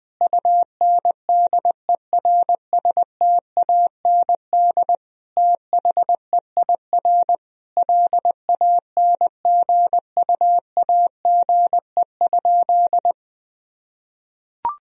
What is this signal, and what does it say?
Morse code, heard as a keyed tone, about 21 wpm, 700 Hz. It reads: UNDERSTAND THEIR LANGUAGE?
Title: Sentences from Top 1000 Words - Review Part 3 - 20wpm